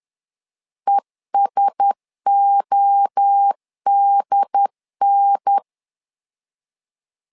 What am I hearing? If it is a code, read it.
Morse code, heard as a keyed tone, about 10 wpm, 800 Hz. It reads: ESODN
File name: LEZIONI / 4 Q1S 10wpm 2133